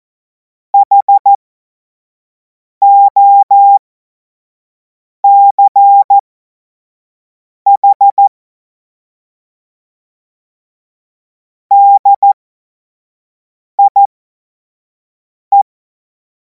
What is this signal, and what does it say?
Morse code, heard as a keyed tone, about 14 wpm, spaced as for 5 wpm, 800 Hz. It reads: HOCH DIE
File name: Morsekurs / Witze / Witz 05. Mai 2026 25 F70